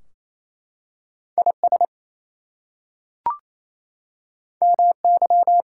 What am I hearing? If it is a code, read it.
Morse code, heard as a keyed tone, about 27 wpm, 700 Hz. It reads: IS MY
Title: Two Letter Words - 28wpm